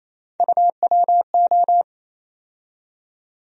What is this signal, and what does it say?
Morse code, heard as a keyed tone, about 28 wpm, 700 Hz. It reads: UWO
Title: USA Call Sign Postfixes - 3 Letters 28wpm